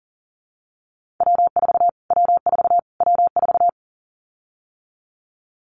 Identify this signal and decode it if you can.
Morse code, heard as a keyed tone, about 40 wpm, 700 Hz. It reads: W4 W4 W4
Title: Callsign Prefixes for Top 25 DXCC - Repeated 3x - 40wpm